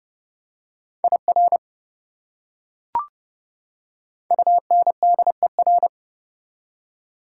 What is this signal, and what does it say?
Morse code, heard as a keyed tone, about 30 wpm, 700 Hz. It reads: IR UNDER
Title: Prefixes and Suffixes 30wpm